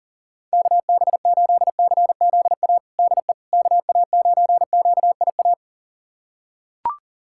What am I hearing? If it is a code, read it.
Morse code, heard as a keyed tone, about 40 wpm, 700 Hz. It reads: KB8CZA DE KA9QIA
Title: USA Callsign DE Callsign 40wpm